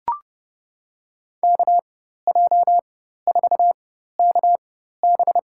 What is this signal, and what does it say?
Morse code, heard as a keyed tone, about 32 wpm, spaced as for 14 wpm, 700 Hz. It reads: KJ4KB
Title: USA Call Signs - 4x Character Spacing - 30wpm